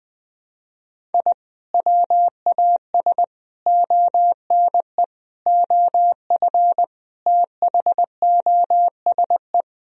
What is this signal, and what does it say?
Morse code, heard as a keyed tone, about 20 wpm, 700 Hz. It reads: I WAS ONE OF THOSE
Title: Sentences from Top 1000 Words - Review Part 1 - 20wpm